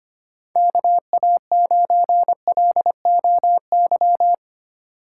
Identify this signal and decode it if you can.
Morse code, heard as a keyed tone, about 25 wpm, 700 Hz. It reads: KA9LOY
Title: USA Call Signs - 6 Characters - 2x Character Spacing - 25wpm